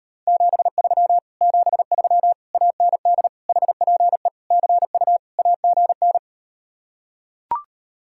Morse code, heard as a keyed tone, about 39 wpm, 700 Hz. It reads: 73 73 AND HPE CU AGN